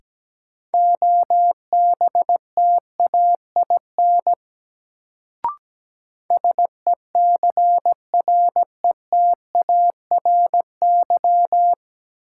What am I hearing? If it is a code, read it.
Morse code, heard as a keyed tone, about 17 wpm, 700 Hz. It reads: OBTAIN SECRETARY
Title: Sentences from Top 1300 Words - New Word Drill - 17wpm